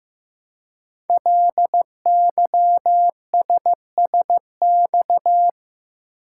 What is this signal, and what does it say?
Morse code, heard as a keyed tone, about 15 wpm, 700 Hz. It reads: LYSSX
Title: Code Groups - 5 Letters - 15wpm